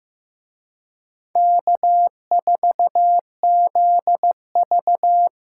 Morse code, heard as a keyed tone, about 18 wpm, 700 Hz. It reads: K4ZV
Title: Speed Racing - USA 1x2 Call Signs 15wpm